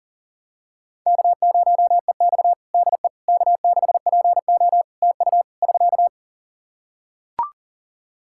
Morse code, heard as a keyed tone, about 40 wpm, 700 Hz. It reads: K0EX DE K6PO TU <SK>
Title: Practice QSO - Exchange 4 - Ending - 40wpm